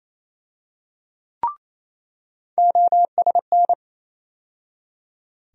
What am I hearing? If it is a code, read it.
Morse code, heard as a keyed tone, about 28 wpm, 700 Hz. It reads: OSN
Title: USA Call Sign Postfixes - 3 Letters 28wpm